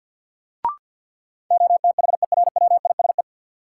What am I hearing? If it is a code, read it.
Morse code, heard as a keyed tone, about 53 wpm, 700 Hz. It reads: OTHERWISE
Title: Sentences from Top 1200 Words - New Word Drill - 50wpm